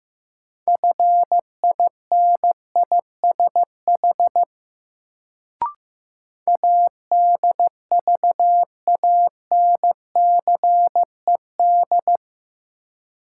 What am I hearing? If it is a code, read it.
Morse code, heard as a keyed tone, about 15 wpm, 700 Hz. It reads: FINISH ADVANCED
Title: Sentences from Top 1200 Words - New Word Drill Alt - 15wpm